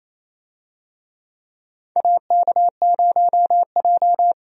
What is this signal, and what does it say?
Morse code, heard as a keyed tone, about 28 wpm, 700 Hz. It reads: AK0J
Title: USA Call Signs - 4 Characters - 28wpm